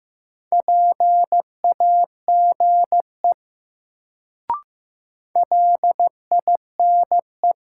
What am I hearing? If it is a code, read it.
Morse code, heard as a keyed tone, about 15 wpm, 700 Hz. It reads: PAGE LINE
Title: Sentences from Top 1000 Words - Review Part 1 - New Word Drill - 15wpm